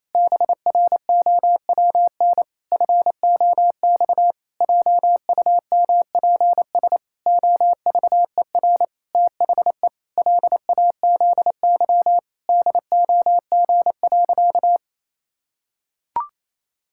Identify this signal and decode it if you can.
Morse code, heard as a keyed tone, about 28 wpm, 700 Hz. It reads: BROWN FOX JUMPS OVER THE LAZY DOG.
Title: Pangram Challenge v2 - 28wpm